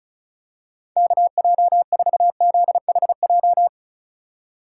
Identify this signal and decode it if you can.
Morse code, heard as a keyed tone, about 34 wpm, 700 Hz. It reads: KJ4ZHJ
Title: USA Call Signs - 6 Characters - 2x Character Spacing - 35wpm